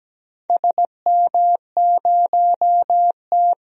Morse code, heard as a keyed tone, about 17 wpm, 700 Hz. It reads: SM0T
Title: DXCC-100 Call Signs - 4 Characters - 17wpm